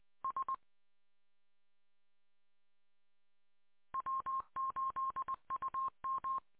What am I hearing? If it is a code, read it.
Morse code, heard as a keyed tone, about 20 wpm, 1060 Hz. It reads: S W8UM